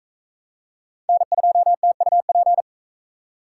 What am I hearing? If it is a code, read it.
Morse code, heard as a keyed tone, about 40 wpm, 700 Hz. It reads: N2TUP